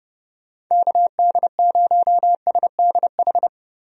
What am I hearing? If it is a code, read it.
Morse code, heard as a keyed tone, about 32 wpm, 700 Hz. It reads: KD0SDH
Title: USA Call Signs - 6 Characters - Repeated 3x - 30wpm